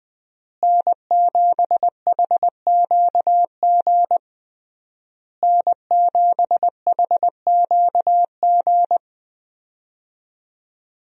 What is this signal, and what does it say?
Morse code, heard as a keyed tone, about 20 wpm, 700 Hz. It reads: N7HQG N7HQG
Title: USA Call Signs - 5 Characters - Repeated 3x - 20wpm